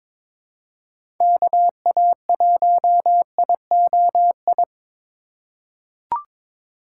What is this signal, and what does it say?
Morse code, heard as a keyed tone, about 22 wpm, 700 Hz. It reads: KA1IOI
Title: USA Call Signs - 6 Characters - 3x Character Spacing - 22wpm